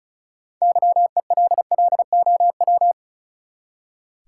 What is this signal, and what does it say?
Morse code, heard as a keyed tone, about 36 wpm, 700 Hz. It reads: YELLOW
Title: Six Letter Words - 35wpm